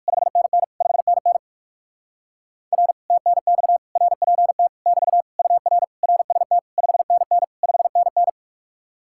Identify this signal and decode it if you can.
Morse code, heard as a keyed tone, about 60 wpm, 700 Hz. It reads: 5NN 5NN R TNX RPT <BT> UR RST 5NN 5NN